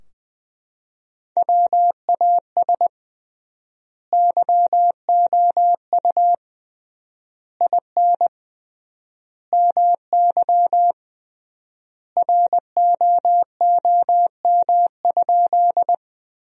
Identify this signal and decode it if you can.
Morse code, heard as a keyed tone, about 21 wpm, 700 Hz. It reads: WAS YOU IN MY ROOM?
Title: 3x Word-Spacing - Sentences from Top 300 Words - 20wpm